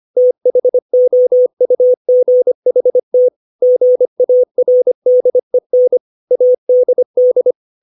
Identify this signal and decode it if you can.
Morse code, heard as a keyed tone, about 25 wpm, 500 Hz. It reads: THOUGHT GARDEN ADD
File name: 632 words / 632 Words 25WPM